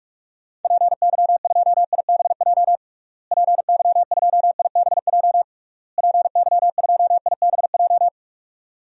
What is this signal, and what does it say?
Morse code, heard as a keyed tone, about 41 wpm, 700 Hz. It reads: PY2IBJ PY2IBJ PY2IBJ
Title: DXCC-100 Call Signs - 6 Characters - Repeated 3x - 45wpm